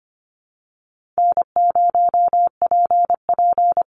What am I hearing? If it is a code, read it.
Morse code, heard as a keyed tone, about 25 wpm, 700 Hz. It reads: N0PP
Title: Easy Call Signs Repeated 3x - 25wpm